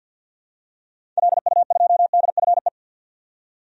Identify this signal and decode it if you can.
Morse code, heard as a keyed tone, about 52 wpm, 700 Hz. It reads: LU2DFE
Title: DXCC-100 Call Signs - 6 Characters - 50wpm